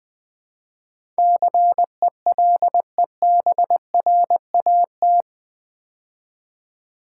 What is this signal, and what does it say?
Morse code, heard as a keyed tone, about 20 wpm, 700 Hz. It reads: CELEBRAT
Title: Build A Word - 4 Letters At A Time - 11-Letter Words - 20wpm